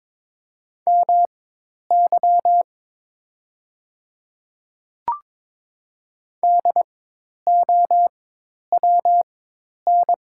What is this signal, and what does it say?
Morse code, heard as a keyed tone, about 22 wpm, spaced as for 10 wpm, 700 Hz. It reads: MY DOWN
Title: 4x Character Spacing - Top 200 Words - 22wpm